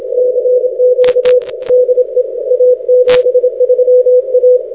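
Morse code, heard as a keyed tone, about 27 wpm, 500 Hz. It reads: CQ DE A73A
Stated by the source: Morse code recorded on the 160 m band.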